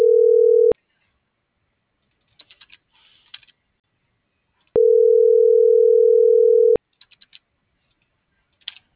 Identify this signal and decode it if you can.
Ringback tone (North America), 440 + 480 Hz.